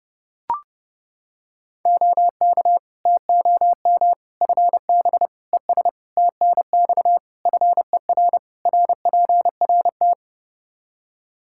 Morse code, heard as a keyed tone, about 30 wpm, 700 Hz. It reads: OK TOM FB ES TNX FER RPRT
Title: Practice QSO - Exchange 2 - Info - 30wpm